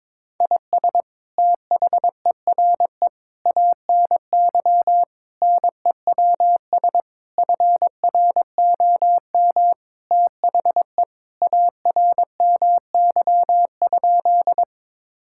Morse code, heard as a keyed tone, about 22 wpm, 700 Hz. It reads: IS THERE ANY NEWS FROM THE ARMY?
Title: Sentences from Top 1000 Words - Review Part 5 - 22wpm